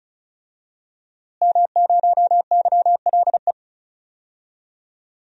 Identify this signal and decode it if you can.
Morse code, heard as a keyed tone, about 36 wpm, 700 Hz. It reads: M0YLE
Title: DXCC-100 Call Signs - 5 Characters - Repeated 2x - 35wpm